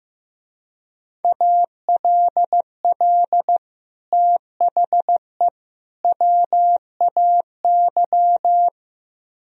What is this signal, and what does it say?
Morse code, heard as a keyed tone, about 17 wpm, 700 Hz. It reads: ALL THE WAY
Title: Speed-Racing - Sets of 3 Words 15wpm